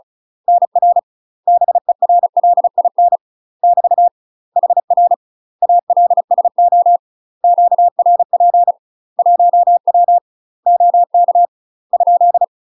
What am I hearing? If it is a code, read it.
Morse code, heard as a keyed tone, about 35 wpm, 700 Hz. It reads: NR BERLIN <BT> HR ALSO QRP 1W OK ?
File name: UK MASTER QSOs TRAINING / 35WPM / UK MASTER QSOs 0001